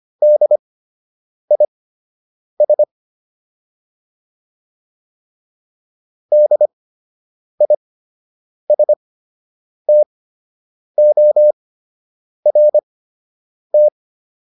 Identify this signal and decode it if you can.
Morse code, heard as a keyed tone, about 25 wpm, spaced as for 8 wpm, 600 Hz. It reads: DIS DISTORT